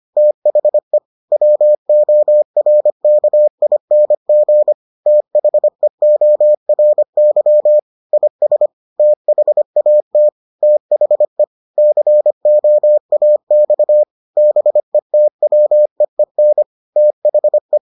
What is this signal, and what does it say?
Morse code, heard as a keyed tone, about 25 wpm, 600 Hz. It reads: THE WORKING THEORY IS THAT THE COAX BETWEEN THE